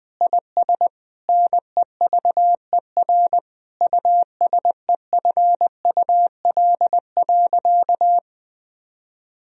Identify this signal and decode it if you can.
Morse code, heard as a keyed tone, about 21 wpm, 700 Hz. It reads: IS NEVER USEFUL.
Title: Sentences from Top 1000 Words - Review Part 4 - 20wpm